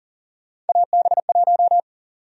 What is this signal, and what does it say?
Morse code, heard as a keyed tone, about 40 wpm, 700 Hz. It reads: AB1
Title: USA Call Sign Prefixes - 40wpm